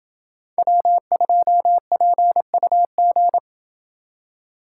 Morse code, heard as a keyed tone, about 27 wpm, 700 Hz. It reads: W2PUG